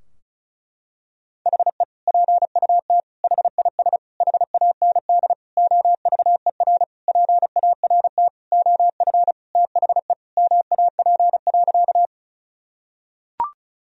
Morse code, heard as a keyed tone, about 34 wpm, 700 Hz. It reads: HE PUT HIS HAND OVER PART OF THE MAP.
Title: Sentences from Top 1000 Words - Review Part 2 - 35wpm